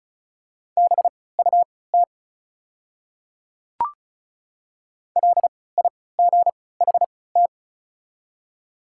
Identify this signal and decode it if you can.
Morse code, heard as a keyed tone, about 36 wpm, spaced as for 20 wpm, 700 Hz. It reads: BUT LIGHT
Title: 3x Character Spacing - Top 200 Words - 35wpm